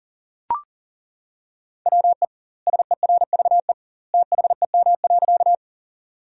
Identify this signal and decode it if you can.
Morse code, heard as a keyed tone, about 40 wpm, 700 Hz. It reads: WE SERVE THEM.
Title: 2x Word-Spacing - Sentences from Top 400 Words - 40wpm